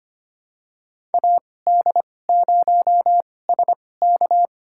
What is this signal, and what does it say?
Morse code, heard as a keyed tone, about 25 wpm, spaced as for 18 wpm, 700 Hz. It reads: AD0SK
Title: USA Call Signs - 2x Character Spacing - 25wpm